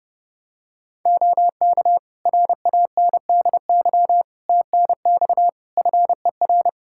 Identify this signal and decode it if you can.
Morse code, heard as a keyed tone, about 30 wpm, 700 Hz. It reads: OK RANDY TNX FER
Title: Practice QSO - Exchange 4 - Ending - 30wpm